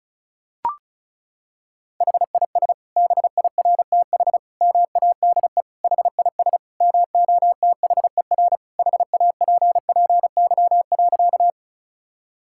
Morse code, heard as a keyed tone, about 34 wpm, 700 Hz. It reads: HIS BIRTH MADE HIS MOTHER HAPPY.
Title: Sentences from Top 1000 Words - Review Part 5 - 35wpm